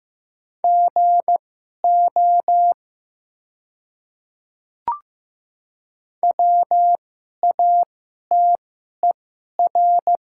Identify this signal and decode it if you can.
Morse code, heard as a keyed tone, about 15 wpm, spaced as for 11 wpm, 690 Hz. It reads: GO WATER
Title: 2x Character Spacing - Top 100 Words - 15wpm